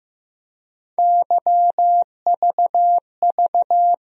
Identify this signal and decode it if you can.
Morse code, heard as a keyed tone, about 15 wpm, 700 Hz. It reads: YVV
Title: USA Call Sign Postfixes - 3 Letters 15wpm